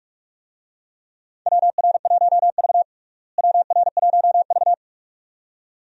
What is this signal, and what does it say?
Morse code, heard as a keyed tone, about 47 wpm, 700 Hz. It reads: WR1V WR1V
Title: USA Call Signs - 4 Characters - Repeated 2x - 45wpm